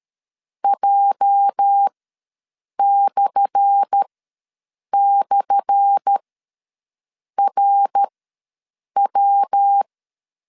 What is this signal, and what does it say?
Morse code, heard as a keyed tone, about 13 wpm, spaced as for 7 wpm, 800 Hz. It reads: J//RW
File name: LEZIONI / 10 SIGNS / 10A SLASH ws 12wpm 3443